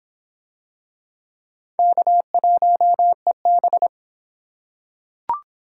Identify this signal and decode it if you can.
Morse code, heard as a keyed tone, about 23 wpm, 700 Hz. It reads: K1EB